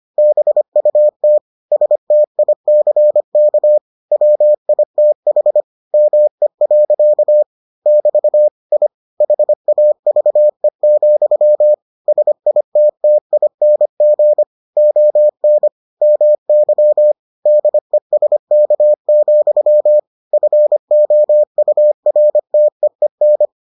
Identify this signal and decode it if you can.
Morse code, heard as a keyed tone, about 25 wpm, 600 Hz. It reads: BUT STICK WITH ME. <BT> I HAVE, SITTING ON MY DESK, FOURTEEN